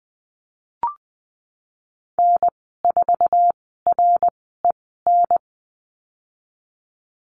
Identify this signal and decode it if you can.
Morse code, heard as a keyed tone, about 19 wpm, spaced as for 14 wpm, 700 Hz. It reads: N4REN
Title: 076 Lesson 13g - Letter L - Call Signs 20wpm